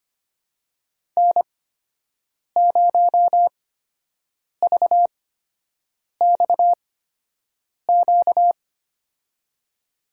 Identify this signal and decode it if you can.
Morse code, heard as a keyed tone, about 25 wpm, spaced as for 7 wpm, 700 Hz. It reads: N0VXQ
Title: USA Call Signs - 5 Characters - 8x Character Spacing - 25wpm